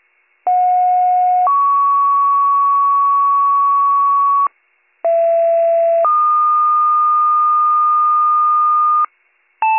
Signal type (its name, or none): two-tone page